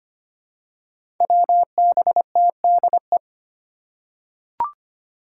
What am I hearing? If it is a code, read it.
Morse code, heard as a keyed tone, about 25 wpm, 700 Hz. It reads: WBTDE
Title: Five Challenging Letter-Number Combinations 25wpm